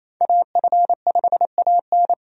Morse code, heard as a keyed tone, about 27 wpm, 700 Hz. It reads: AF5AN
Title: USA Call Signs - 5 Characters - 28wpm